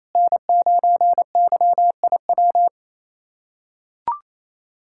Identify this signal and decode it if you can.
Morse code, heard as a keyed tone, about 28 wpm, 700 Hz. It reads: N9YIW
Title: USA Callsign DE Callsign - 5 Characters - 28wpm